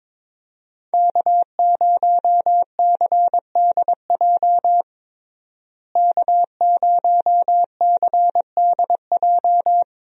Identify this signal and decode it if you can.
Morse code, heard as a keyed tone, about 22 wpm, 700 Hz. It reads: K0CDJ K0CDJ
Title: USA Call Signs - 5 Characters - Repeated 2x - 22wpm